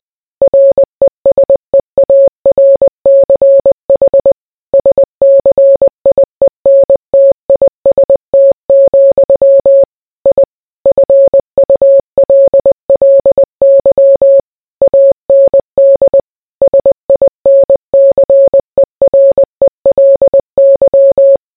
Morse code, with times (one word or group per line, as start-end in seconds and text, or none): 0.4-4.3 RESEARCH
4.7-9.8 SCIENTIST,
10.3-10.4 I
10.9-14.4 FULLY
14.8-16.2 AND
16.6-21.4 SINCERELY